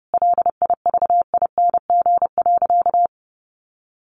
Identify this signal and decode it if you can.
Morse code, heard as a keyed tone, about 30 wpm, 700 Hz. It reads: LIVING.
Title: Sentences from Top 1100 Words 30wpm